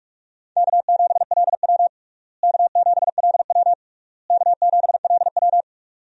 Morse code, heard as a keyed tone, about 46 wpm, 700 Hz. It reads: K7LW K7LW K7LW
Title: USA Call Signs - 4 Characters - Repeated 3x - 45wpm